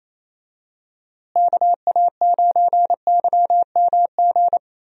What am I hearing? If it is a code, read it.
Morse code, heard as a keyed tone, about 28 wpm, 700 Hz. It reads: KA9YMG
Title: USA Call Signs - 6 Characters - Repeated 3x - 28wpm